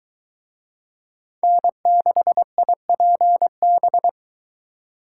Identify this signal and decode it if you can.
Morse code, heard as a keyed tone, about 23 wpm, 700 Hz. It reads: N6IPB